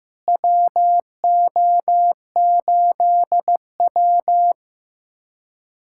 Morse code, heard as a keyed tone, about 15 wpm, 700 Hz. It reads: WO8W
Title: USA Callsign DE Callsign - 4 Characters - 15wpm